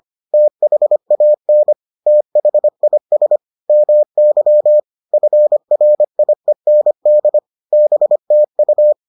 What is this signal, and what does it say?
Morse code, heard as a keyed tone, about 25 wpm, 600 Hz. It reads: THAN THIS MY FRIEND BTU
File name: 25wpm-6